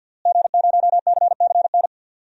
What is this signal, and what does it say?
Morse code, heard as a keyed tone, about 49 wpm, 700 Hz. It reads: G0CCN
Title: DXCC-100 Call Signs - 5 Characters - Repeated 3x - 50wpm